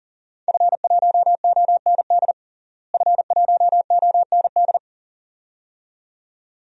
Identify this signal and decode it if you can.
Morse code, heard as a keyed tone, about 37 wpm, 700 Hz. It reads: F1OND F1OND
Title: DXCC-100 Call Signs - 5 Characters - Repeated 2x - 40wpm